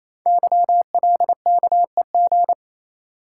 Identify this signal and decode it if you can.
Morse code, heard as a keyed tone, about 27 wpm, 700 Hz. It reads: YLKEG
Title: Code Groups - 5 Letters - 28wpm